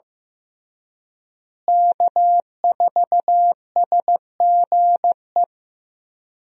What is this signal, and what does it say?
Morse code, heard as a keyed tone, about 15 wpm, 700 Hz. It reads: K4SGE
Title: USA Call Signs - 5 Characters - Repeated 3x - 15wpm